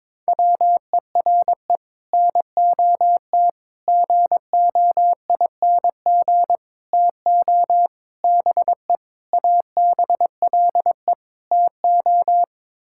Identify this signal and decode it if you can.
Morse code, heard as a keyed tone, about 22 wpm, 700 Hz. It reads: WERE NOT GOING TO BE ABLE TO
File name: practice / 22wpm / Sets of 7 Words - 22wpm